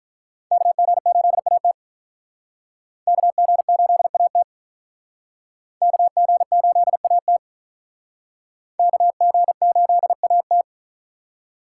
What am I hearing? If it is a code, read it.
Morse code, heard as a keyed tone, about 46 wpm, 700 Hz. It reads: KG8AT KG8AT KG8AT KG8AT